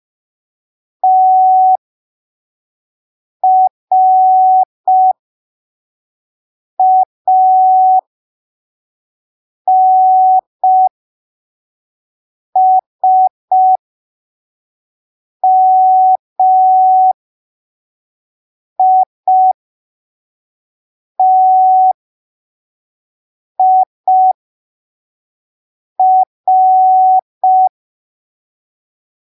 Morse code, heard as a keyed tone, about 5 wpm, spaced as for 3 wpm, 750 Hz. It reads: TRANSMITIR
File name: arquivos / texto / texto 04 5ppm